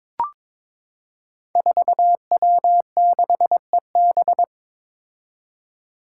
Morse code, heard as a keyed tone, about 22 wpm, 700 Hz. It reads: 4W6EB
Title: Five Challenging Letter-Number Combinations 22wpm